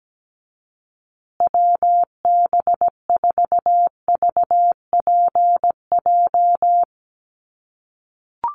Morse code, heard as a keyed tone, about 17 wpm, 700 Hz. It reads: WB4VPJ
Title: USA Call Signs Repeated 2x 17wpm